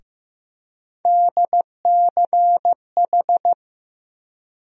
Morse code, heard as a keyed tone, about 15 wpm, 700 Hz. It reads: DCH
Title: USA Call Sign Postfixes - 3 Letters 15wpm